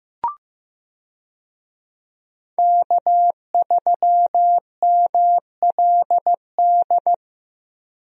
Morse code, heard as a keyed tone, about 15 wpm, 700 Hz. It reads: K3MLD
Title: USA Call Signs - 5 Characters - Repeated 2x - 15wpm